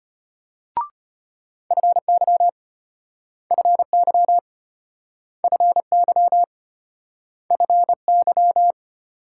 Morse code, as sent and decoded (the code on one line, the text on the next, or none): ..-. -.-- / ..-. -.-- / ..-. -.-- / ..-. -.--
FY FY FY FY